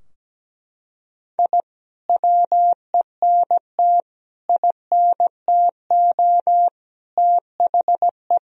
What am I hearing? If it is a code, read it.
Morse code, heard as a keyed tone, about 17 wpm, 700 Hz. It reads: I WENT INTO THE
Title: Sentences from Top 1000 Words - Review Part 4 - 17wpm